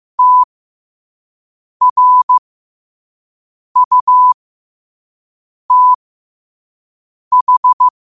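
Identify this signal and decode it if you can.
Morse code, heard as a keyed tone, about 15 wpm, spaced as for 5 wpm, 1000 Hz. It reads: TRUTH
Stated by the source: Morse practice quote sent at 5 wpm.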